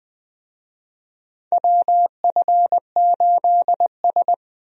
Morse code, heard as a keyed tone, about 20 wpm, 700 Hz. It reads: WF8S
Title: USA Call Signs - 4 Characters - 20wpm